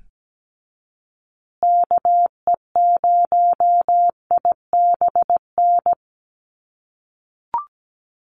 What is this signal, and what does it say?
Morse code, heard as a keyed tone, about 17 wpm, 700 Hz. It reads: KE0IBN
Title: USA Call Signs Repeated 3x 17wpm